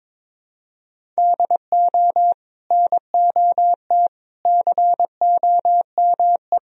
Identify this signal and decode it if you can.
Morse code, heard as a keyed tone, about 22 wpm, 700 Hz. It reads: DO NOT COME
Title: Sentences from Top 1000 Words - Review Part 1 - 22wpm